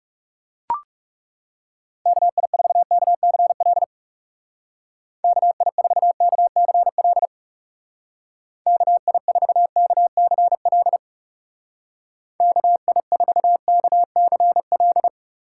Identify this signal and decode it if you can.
Morse code, heard as a keyed tone, about 35 wpm, 700 Hz. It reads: KI4KCL KI4KCL KI4KCL KI4KCL
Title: Speed Racing - USA Call Signs 30wpm